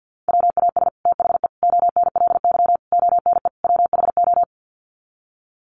Morse code, heard as a keyed tone, about 50 wpm, 700 Hz. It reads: WAS THE ONLY ONE WHO
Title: Sets of 5 Words Encore2 50wpm